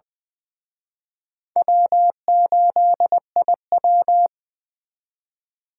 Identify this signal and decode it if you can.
Morse code, heard as a keyed tone, about 24 wpm, 700 Hz. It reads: W8IW
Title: Speed Racing - USA 1x2 Call Signs 20wpm